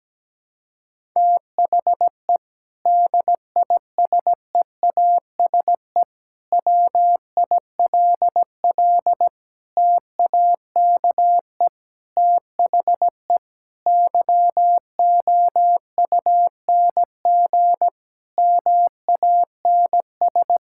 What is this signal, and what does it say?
Morse code, heard as a keyed tone, about 17 wpm, 700 Hz. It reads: THE DISEASE WILL TAKE THE YOUNG MANS
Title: Sentences from Top 1000 Words - Review Part 6 - 17wpm